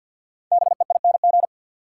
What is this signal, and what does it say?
Morse code, heard as a keyed tone, about 50 wpm, 700 Hz. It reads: BEING
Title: Sets of 4 Words - New Word Drill - 50wpm